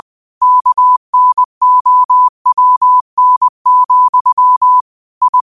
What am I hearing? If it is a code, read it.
Morse code, heard as a keyed tone, about 20 wpm, 1000 Hz. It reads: KNOWN, I